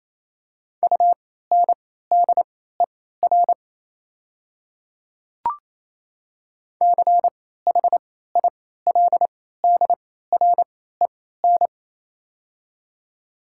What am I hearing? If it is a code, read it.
Morse code, heard as a keyed tone, about 28 wpm, spaced as for 16 wpm, 700 Hz. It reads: UNDER CHILDREN
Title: 3x Character Spacing - Top 300 Words - 28wpm